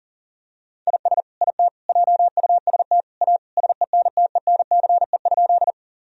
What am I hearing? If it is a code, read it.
Morse code, heard as a keyed tone, about 40 wpm, 700 Hz. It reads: IS IT JUST A SENTENCE?
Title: Sentences from Top 1000 Words - Review Part 1 - 40wpm